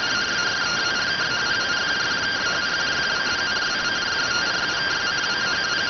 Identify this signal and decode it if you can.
Two-tone teleprinter signal, about 170 Hz shift.